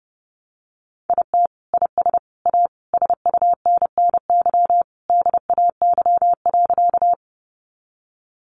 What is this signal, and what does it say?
Morse code, heard as a keyed tone, about 30 wpm, 700 Hz. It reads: IT IS A SUNNY DAY.
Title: Additional Easy Phrases 30wpm